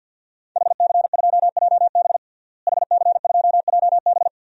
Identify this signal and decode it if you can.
Morse code, heard as a keyed tone, about 52 wpm, 700 Hz. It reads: HC2JB HC2JB
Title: DXCC-100 Call Signs - 5 Characters - Repeated 2x - 50wpm